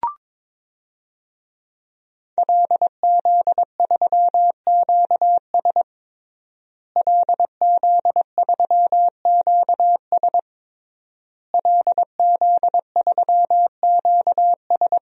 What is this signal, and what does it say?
Morse code, heard as a keyed tone, about 22 wpm, 700 Hz. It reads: LZ3QS LZ3QS LZ3QS
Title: DXCC-100 Call Signs - 5 Characters - Repeated 3x - 22wpm